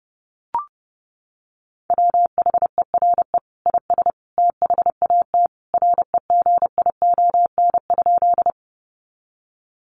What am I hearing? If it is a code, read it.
Morse code, heard as a keyed tone, about 30 wpm, 700 Hz. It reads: WHERE IS THAT REGION?
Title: Sentences from Top 1500 Words 30wpm